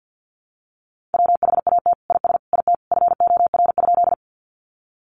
Morse code, heard as a keyed tone, about 50 wpm, 700 Hz. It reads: WHAT IS IT FOR?
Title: Sentences from Top 100 Words 50wpm